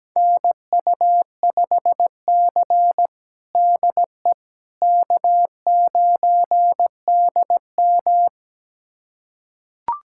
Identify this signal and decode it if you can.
Morse code, heard as a keyed tone, about 17 wpm, 700 Hz. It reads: NU5C DE K9DM
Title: USA Callsign DE Callsign - 4 Characters - 17wpm